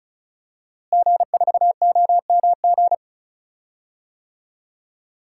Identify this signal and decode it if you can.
Morse code, heard as a keyed tone, about 34 wpm, 690 Hz. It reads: G4OMG
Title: DXCC-100 Call Signs - 5 Characters - Repeated 3x - 35wpm